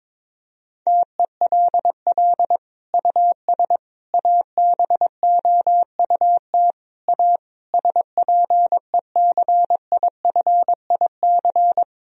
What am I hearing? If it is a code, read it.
Morse code, heard as a keyed tone, about 22 wpm, 700 Hz. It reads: TELL US ABOUT A SPECIFIC
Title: Sentences from Top 1000 Words - Review Part 3 - 22wpm